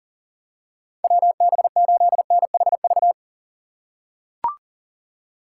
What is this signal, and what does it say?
Morse code, heard as a keyed tone, about 40 wpm, 700 Hz. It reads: WB8NHV
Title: USA Call Signs - 2x Character Spacing - 40wpm